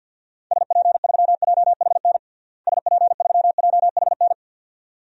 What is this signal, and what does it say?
Morse code, heard as a keyed tone, about 50 wpm, 700 Hz. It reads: SP3JHN SP3JHN
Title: DXCC-100 Call Signs - 6 Characters - Repeated 2x - 50wpm